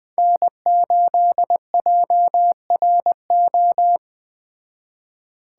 Morse code, heard as a keyed tone, about 20 wpm, 700 Hz. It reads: N8JRO
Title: Speed Racing - USA 1x3 Call Signs 17wpm